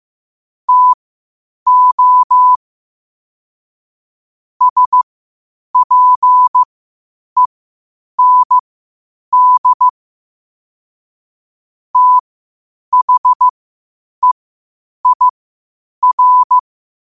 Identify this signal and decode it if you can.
Morse code, heard as a keyed tone, about 15 wpm, spaced as for 8 wpm, 1000 Hz. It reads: TO SPEND THEIR
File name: qotdfiles / 8wpm